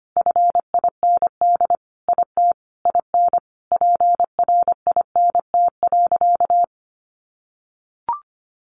Morse code, heard as a keyed tone, about 25 wpm, 700 Hz. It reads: FIND IT IN PRINT.
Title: Sentences from Top 1400 Words 25wpm